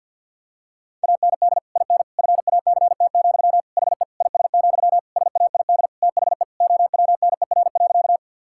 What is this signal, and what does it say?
Morse code, heard as a keyed tone, about 51 wpm, 700 Hz. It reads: AND IN FACT, HE IS, SAID THE OWNER.
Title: Sentences from Top 1600 Words 50wpm